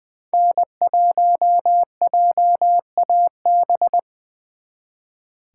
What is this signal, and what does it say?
Morse code, heard as a keyed tone, about 24 wpm, 700 Hz. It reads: N1JAB
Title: Speed Racing - USA Call Signs 20wpm